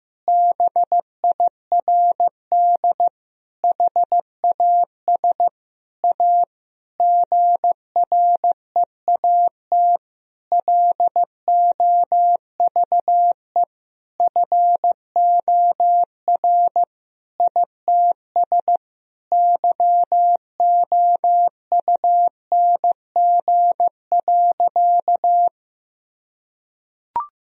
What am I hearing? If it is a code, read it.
Morse code, heard as a keyed tone, about 15 wpm, 700 Hz. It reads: BIRD HAS A GREAT LOVE FOR ITS YOUNG.
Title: Sentences from Top 1000 Words - Review Part 2 - 15wpm